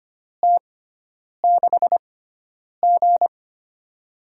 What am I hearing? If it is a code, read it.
Morse code, heard as a keyed tone, about 25 wpm, spaced as for 9 wpm, 700 Hz. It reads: T6G
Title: USA Call Signs - 4 Characters - 6x Character Spacing - 25wpm